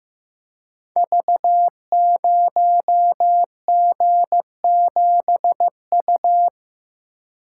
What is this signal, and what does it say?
Morse code, heard as a keyed tone, about 15 wpm, 700 Hz. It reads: V0G7U
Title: Code Groups - 5 Letter:Numbers - 15wpm